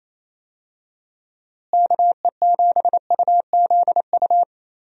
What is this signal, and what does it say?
Morse code, heard as a keyed tone, about 28 wpm, 700 Hz. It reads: KE7UZU
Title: USA Call Signs - 6 Characters - Repeated 3x - 28wpm